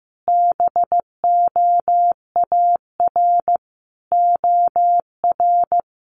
Morse code, heard as a keyed tone, about 15 wpm, 700 Hz. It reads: BOAR OR
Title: A Sound Of Thunder 15wpm